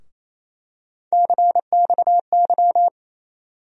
Morse code, heard as a keyed tone, about 28 wpm, 700 Hz. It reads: CXY
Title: USA Call Sign Postfixes - 3 Letters 28wpm